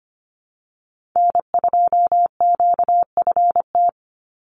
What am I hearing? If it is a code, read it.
Morse code, heard as a keyed tone, about 25 wpm, 690 Hz. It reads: N2QFT
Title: USA Call Signs 25wpm